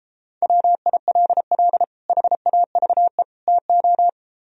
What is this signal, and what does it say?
Morse code, heard as a keyed tone, about 29 wpm, 700 Hz. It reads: WILL HAVE TO